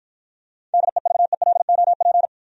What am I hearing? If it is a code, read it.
Morse code, heard as a keyed tone, about 49 wpm, 700 Hz. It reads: DEVELOP